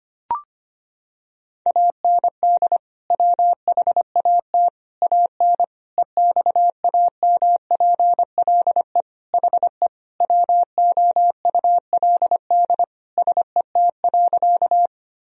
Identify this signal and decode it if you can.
Morse code, heard as a keyed tone, about 25 wpm, 700 Hz. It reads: AND WHAT AN EXAMPLE HE WOULD SET.
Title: Sentences from Top 1000 Words - Review Part 1 - 25wpm